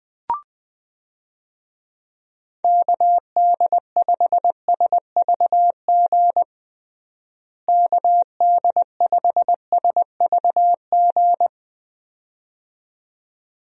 Morse code, heard as a keyed tone, about 20 wpm, 700 Hz. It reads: KD5SVG KD5SVG
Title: USA Call Signs - 6 Characters - Repeated 2x - 20wpm